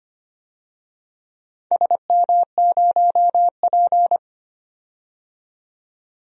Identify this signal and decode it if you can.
Morse code, heard as a keyed tone, about 25 wpm, 700 Hz. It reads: SM0P
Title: DXCC-100 Call Signs - 4 Characters - 25wpm